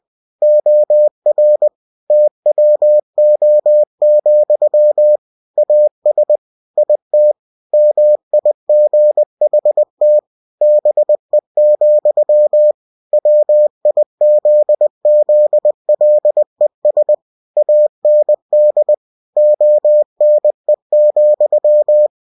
Morse code, heard as a keyed tone, about 20 wpm, 600 Hz. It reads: OR TWO, AS IT MIGHT BE, WIZZLES AND ONE,
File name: cw-books / winnie-the-pooh / winnie-20wpm-page-0023